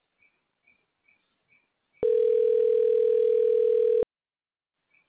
Ringback tone (North America), 440 + 480 Hz.